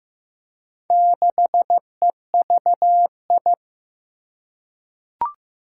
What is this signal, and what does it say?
Morse code, heard as a keyed tone, about 15 wpm, 700 Hz. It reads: 6EVI
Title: Four Challenging Letter-Number Combinations 15wpm